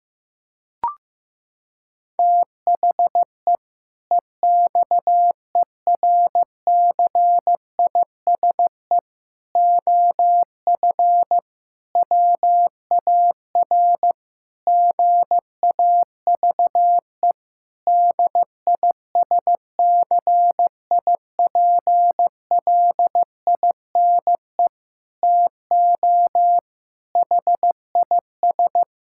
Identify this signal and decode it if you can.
Morse code, heard as a keyed tone, about 15 wpm, 700 Hz. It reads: THE EXERCISE OF WAR GAVE DISCIPLINE TO HIS
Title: Sentences from Top 1000 Words - Review Part 6 - 15wpm